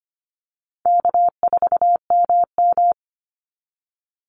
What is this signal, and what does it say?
Morse code, heard as a keyed tone, about 25 wpm, 700 Hz. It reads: K4MM
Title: Easy Call Signs - 25wpm